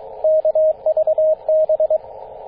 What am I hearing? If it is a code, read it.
Morse code, heard as a keyed tone, about 22 wpm, 630 Hz, with radio noise. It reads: KVB
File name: 9m0o-24-891mhz-cw